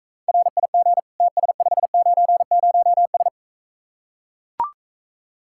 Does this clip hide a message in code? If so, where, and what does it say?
Yes: Morse code from 0.3 to 3.3 s, heard as a keyed tone, about 44 wpm, 700 Hz. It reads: RIG TS590S